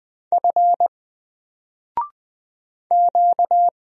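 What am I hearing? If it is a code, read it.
Morse code, heard as a keyed tone, about 20 wpm, 700 Hz. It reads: F Q
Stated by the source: Single letter and number Morse drill at 20 wpm, from N4LCD.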